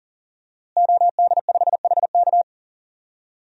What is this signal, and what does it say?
Morse code, heard as a keyed tone, about 40 wpm, 700 Hz. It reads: OD5HK
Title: DXCC-100 Call Signs - 5 Characters - Repeated 3x - 40wpm